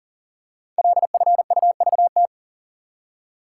Morse code, heard as a keyed tone, about 39 wpm, 700 Hz. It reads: LFUVT
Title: Code Groups - 5 Letters - 40wpm